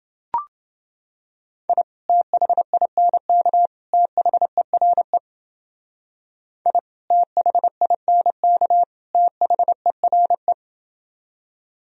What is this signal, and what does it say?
Morse code, heard as a keyed tone, about 29 wpm, 700 Hz. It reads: I THINK THERE I THINK THERE